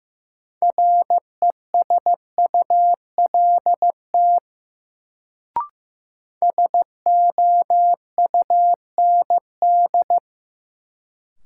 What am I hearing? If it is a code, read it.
Morse code, heard as a keyed tone, about 16 wpm, 700 Hz. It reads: RESULT SOUND
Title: Sets of 3 Words - New Word Drill - 15wpm